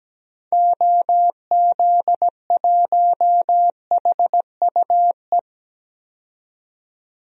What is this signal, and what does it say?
Morse code, heard as a keyed tone, about 17 wpm, 700 Hz. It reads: OZ1HUE
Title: DXCC-100 Call Signs - 6 Characters - 17wpm